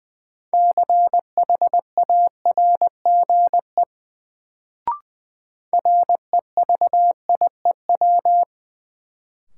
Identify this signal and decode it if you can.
Morse code, heard as a keyed tone, about 20 wpm, 700 Hz. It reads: CHARGE REVIEW
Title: Sentences from Top 900 Words - New Word Drill - 20wpm